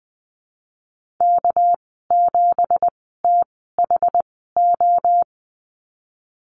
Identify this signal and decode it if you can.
Morse code, heard as a keyed tone, about 19 wpm, spaced as for 14 wpm, 700 Hz. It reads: K7THO
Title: 245 Lesson 39g - Letter z - Call Signs 20wpm